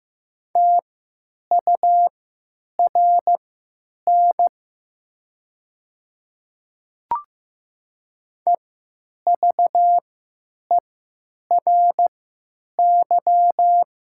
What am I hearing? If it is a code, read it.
Morse code, heard as a keyed tone, about 15 wpm, spaced as for 9 wpm, 700 Hz. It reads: TURN EVERY
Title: 3x Character Spacing - Top 200 Words - 15wpm